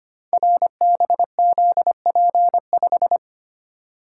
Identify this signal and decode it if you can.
Morse code, heard as a keyed tone, about 25 wpm, 700 Hz. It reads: RBZP5
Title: Code Groups - 5 Letter:Numbers - 25wpm